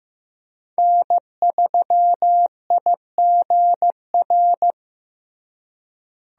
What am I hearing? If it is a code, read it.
Morse code, heard as a keyed tone, about 15 wpm, 700 Hz. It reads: N3IGR
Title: USA Call Signs - 5 Characters - Repeated 3x - 15wpm